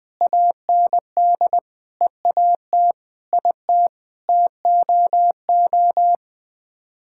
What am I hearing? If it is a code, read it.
Morse code, heard as a keyed tone, about 21 wpm, 700 Hz. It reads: AND EAT IT TOO
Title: Sets of 8 Words - 20wpm